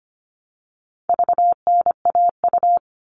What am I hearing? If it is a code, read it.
Morse code, heard as a keyed tone, about 25 wpm, 700 Hz. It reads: VNAU
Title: All Cut Numbers - 4-Digit 25wpm